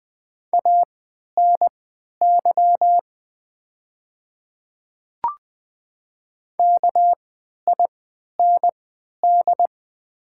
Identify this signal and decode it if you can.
Morse code, heard as a keyed tone, about 20 wpm, spaced as for 11 wpm, 690 Hz. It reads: ANY KIND
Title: 3x Character Spacing - Top 200 Words - 20wpm